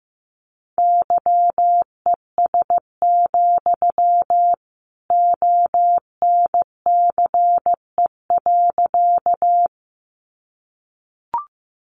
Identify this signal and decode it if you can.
Morse code, heard as a keyed tone, about 15 wpm, 700 Hz. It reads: YES, ONCE.
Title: Sentences from Top 500 Words 15wpm